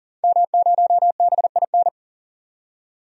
Morse code, heard as a keyed tone, about 40 wpm, 700 Hz. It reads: M0BIN
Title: DXCC-100 Call Signs - 5 Characters - 40wpm